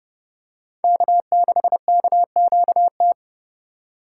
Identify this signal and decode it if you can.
Morse code, heard as a keyed tone, about 30 wpm, 700 Hz. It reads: K6KQT
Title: USA Call Signs - 8x Character Spacing - 30wpm